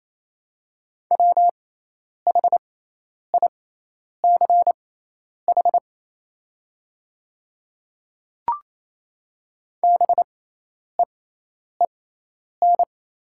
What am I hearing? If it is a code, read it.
Morse code, heard as a keyed tone, about 28 wpm, spaced as for 10 wpm, 700 Hz. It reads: WHICH BEEN
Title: 6x Character Spacing - Top 100 Words - 28wpm